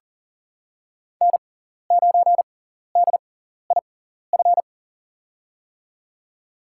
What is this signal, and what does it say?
Morse code, heard as a keyed tone, about 40 wpm, spaced as for 14 wpm, 700 Hz. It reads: N9DIF
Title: USA Call Signs - 5 Characters - 6x Character Spacing - 40wpm